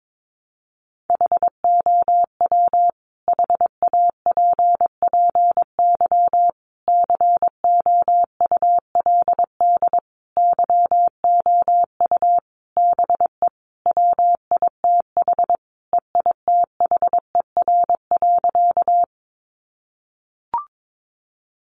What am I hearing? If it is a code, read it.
Morse code, heard as a keyed tone, about 22 wpm, 700 Hz. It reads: HOW HAPPY COULD YOU BE WITH EITHER.
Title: Sentences from Top 1000 Words 22wpm